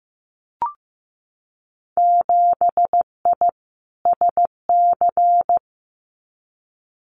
Morse code, heard as a keyed tone, about 15 wpm, 700 Hz. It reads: 7I SC
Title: Winter Field Day Exchanges Encore 15wpm